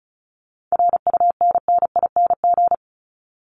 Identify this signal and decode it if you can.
Morse code, heard as a keyed tone, about 34 wpm, 700 Hz. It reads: RUNNING
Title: ING Suffix - Top 100 Words with Base Words 35wpm